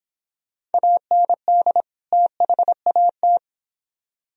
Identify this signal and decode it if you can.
Morse code, heard as a keyed tone, about 30 wpm, 700 Hz. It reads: AND THAT